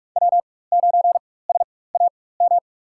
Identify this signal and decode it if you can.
Morse code, heard as a keyed tone, about 46 wpm, spaced as for 21 wpm, 700 Hz. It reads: W9SAM
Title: USA Call Signs - 4x Character Spacing - 45wpm